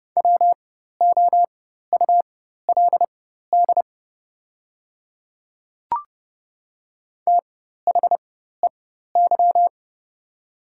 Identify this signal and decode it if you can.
Morse code, heard as a keyed tone, about 30 wpm, spaced as for 14 wpm, 700 Hz. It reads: WOULD THEY
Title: 4x Character Spacing - Top 200 Words - 30wpm